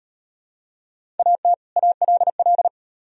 Morse code, heard as a keyed tone, about 35 wpm, 700 Hz. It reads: AT ALL